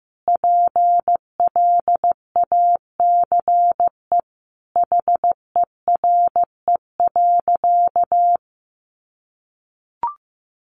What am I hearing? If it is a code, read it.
Morse code, heard as a keyed tone, about 14 wpm, 700 Hz. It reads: PLACE HERE.
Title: Sentences from Top 1500 Words 15wpm